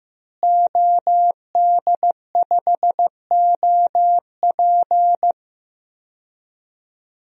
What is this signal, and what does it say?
Morse code, heard as a keyed tone, about 15 wpm, 700 Hz. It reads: OD5OP
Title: DXCC-100 Call Signs - 5 Characters - Repeated 2x - 15wpm